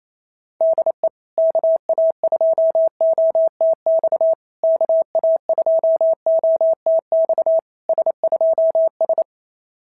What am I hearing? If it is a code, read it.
Morse code, heard as a keyed tone, about 27 wpm, 650 Hz. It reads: DE KA2OTX KA2OTX S2S
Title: SOTA QSO Practice 28wpm